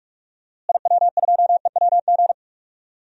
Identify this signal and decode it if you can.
Morse code, heard as a keyed tone, about 45 wpm, 700 Hz. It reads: IW2EWG
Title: DXCC-100 Call Signs - 6 Characters - Repeated 3x - 45wpm